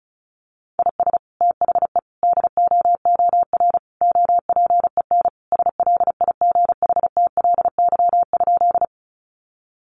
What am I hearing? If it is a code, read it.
Morse code, heard as a keyed tone, about 34 wpm, 700 Hz. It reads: IS THE DOOR OPEN SLIGHTLY?
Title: Sentences from Top 1300 Words 35wpm